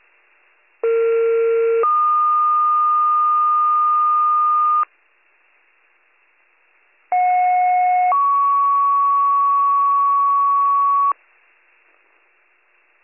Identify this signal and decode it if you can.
Two-tone page, heard as a one-second tone, then a three-second tone, about 460 then 1220 Hz, about 730 then 1120 Hz. It is heard from 0.8 to 11.1 s.